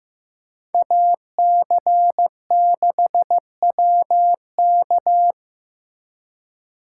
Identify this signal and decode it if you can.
Morse code, heard as a keyed tone, about 18 wpm, 700 Hz. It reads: AC6WK
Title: Speed Racing - USA 2x2 Call Signs 15wpm